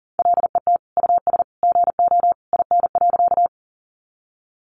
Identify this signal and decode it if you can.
Morse code, heard as a keyed tone, about 40 wpm, 700 Hz. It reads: LET US GO IN.
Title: Sentences from Top 300 Words 40wpm